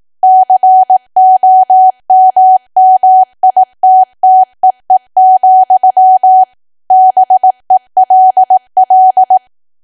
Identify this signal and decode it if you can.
Morse code, heard as a keyed tone, about 18 wpm, 750 Hz. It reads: COMMITTEE, BELL
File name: Archive / 18 WPM / 18WPM